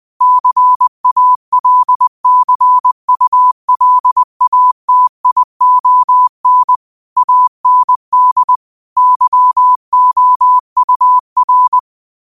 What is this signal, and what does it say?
Morse code, heard as a keyed tone, about 20 wpm, 1000 Hz. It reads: CALCULATION AND YOUR